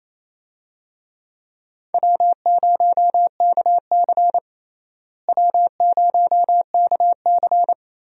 Morse code, heard as a keyed tone, about 28 wpm, 700 Hz. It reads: W0KC W0KC
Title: USA Call Signs - 4 Characters - Repeated 3x - 28wpm